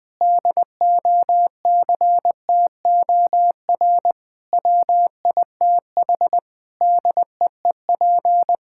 Morse code, heard as a keyed tone, about 20 wpm, 700 Hz. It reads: DOCTOR WITH DEEP
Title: Sentences from Top 1600 Words 20wpm